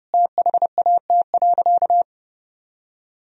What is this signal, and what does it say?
Morse code, heard as a keyed tone, about 30 wpm, 700 Hz. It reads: THAT.
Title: Sentences from Top 1000 Words - Review Part 4 - 30wpm